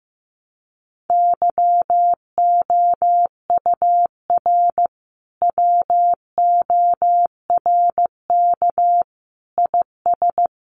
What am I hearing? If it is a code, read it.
Morse code, heard as a keyed tone, about 15 wpm, 700 Hz. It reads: YOUR WORK IS
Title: Sentences from Top 1100 Words 15wpm